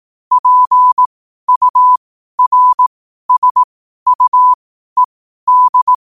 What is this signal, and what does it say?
Morse code, heard as a keyed tone, about 18 wpm, spaced as for 12 wpm, 1000 Hz. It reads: PURSUED
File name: qotdfiles / 12wpm